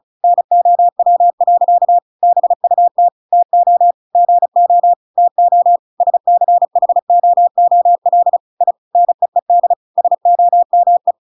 Morse code, heard as a keyed tone, about 35 wpm, 700 Hz. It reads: NOW. BUT TO GO TO SCHOOL I NEED SOME
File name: Pinocchio / Pinocchio ch8 35 35 ik1hge